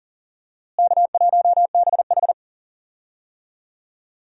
Morse code, heard as a keyed tone, about 37 wpm, 700 Hz. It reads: K1BH
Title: USA Call Signs - 4 Characters - Repeated 3x - 40wpm